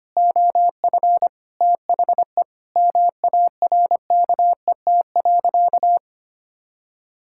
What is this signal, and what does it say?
Morse code, heard as a keyed tone, about 25 wpm, 700 Hz. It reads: OF THE MARKET.
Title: Sentences from Top 1000 Words - Review Part 6 - 25wpm